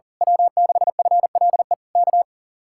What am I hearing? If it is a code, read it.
Morse code, heard as a keyed tone, about 37 wpm, 700 Hz. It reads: W6FLE K
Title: Speed-Racing Common QSO Phrases - 40wpm